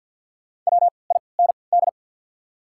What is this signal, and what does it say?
Morse code, heard as a keyed tone, about 53 wpm, spaced as for 29 wpm, 700 Hz. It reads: WIND
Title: 3x Character Spacing - Top 500 Words - 50wpm